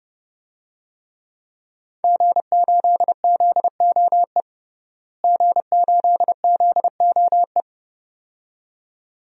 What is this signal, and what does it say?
Morse code, heard as a keyed tone, about 30 wpm, 700 Hz. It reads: G8ZOE G8ZOE
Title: DXCC-100 Call Signs - 5 Characters - Repeated 2x - 30wpm